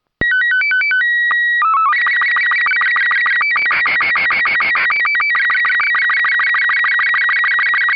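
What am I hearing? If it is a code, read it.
SSTV (slow-scan television), Robot 36.
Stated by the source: SSTV from the autonomous SSTV generator built by F4FPT, fed a test pattern from a FinePix S5500 digital camera.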